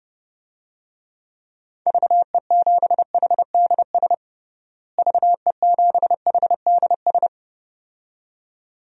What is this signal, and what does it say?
Morse code, heard as a keyed tone, about 30 wpm, 700 Hz. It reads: VE7HDS VE7HDS
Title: DXCC-100 Call Signs - 6 Characters - Repeated 2x - 30wpm